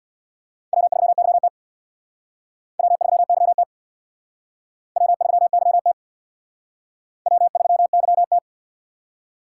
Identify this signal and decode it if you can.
Morse code, heard as a keyed tone, about 65 wpm, 700 Hz. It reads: W3YT W3YT W3YT W3YT